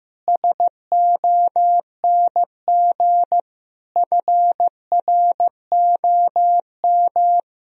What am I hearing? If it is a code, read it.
Morse code, heard as a keyed tone, about 15 wpm, 700 Hz. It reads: SONG FROM
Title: Sentences from Top 1000 Words - Review Part 2 - 15wpm